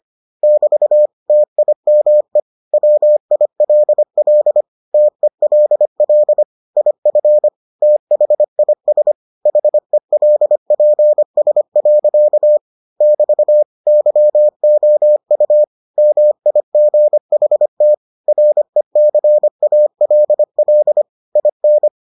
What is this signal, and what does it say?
Morse code, heard as a keyed tone, about 25 wpm, 600 Hz. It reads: <BT> TIME WILL TELL IF THIS HELPS. <BT> YOU MIGHT RECALL IN